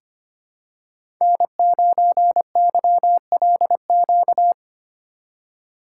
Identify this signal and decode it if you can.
Morse code, heard as a keyed tone, about 25 wpm, 700 Hz. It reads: N9YLQ
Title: USA Call Signs - 5 Characters - 4x Character Spacing - 25wpm